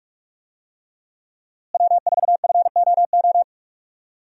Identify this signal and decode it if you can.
Morse code, heard as a keyed tone, about 46 wpm, 700 Hz. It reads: W4FOO
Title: USA Call Signs - 5 Characters - Repeated 3x - 45wpm